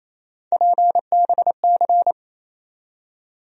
Morse code, heard as a keyed tone, about 28 wpm, 700 Hz. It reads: PBC
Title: USA Call Sign Postfixes - 3 Letters 28wpm